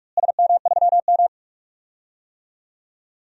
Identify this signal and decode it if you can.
Morse code, heard as a keyed tone, about 43 wpm, 700 Hz. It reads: SM3M
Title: DXCC-100 Call Signs - 4 Characters - Repeated 2x - 45wpm